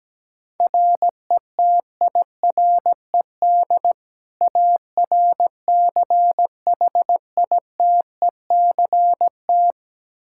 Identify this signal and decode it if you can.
Morse code, heard as a keyed tone, about 17 wpm, 700 Hz. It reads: RETIRED ARCHITECT
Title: Practice QSO - Exchange 3 - More Info - 17wpm